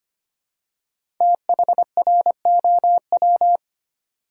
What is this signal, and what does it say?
Morse code, heard as a keyed tone, about 25 wpm, 700 Hz. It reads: THROW
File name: 5 Letter Words - First 3 Letters Given - 25wpm